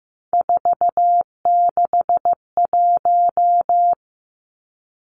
Morse code, heard as a keyed tone, about 15 wpm, 700 Hz. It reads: 461
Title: Three Challenging Letter-Number Combinations 15wpm